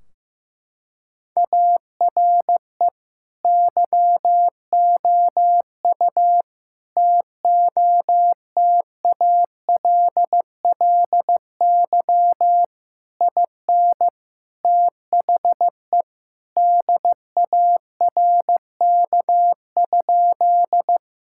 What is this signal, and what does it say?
Morse code, heard as a keyed tone, about 15 wpm, 700 Hz. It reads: ARE YOU TOTALLY IN THE DARK?
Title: Sentences from Top 1600 Words 15wpm